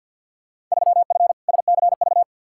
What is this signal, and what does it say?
Morse code, heard as a keyed tone, about 52 wpm, 700 Hz. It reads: 3F SCV
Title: Field Day Exchanges - 50wpm